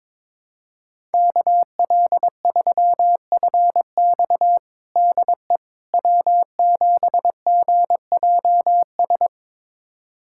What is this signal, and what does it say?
Morse code, heard as a keyed tone, about 21 wpm, 700 Hz. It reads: KL3FX DE W7GJS
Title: USA Callsign DE Callsign - 5 Characters - 22wpm